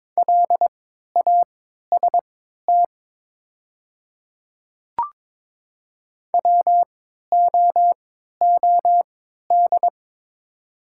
Morse code, heard as a keyed tone, about 22 wpm, spaced as for 12 wpm, 700 Hz. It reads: LAST WOOD
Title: 3x Character Spacing - Top 300 Words - 22wpm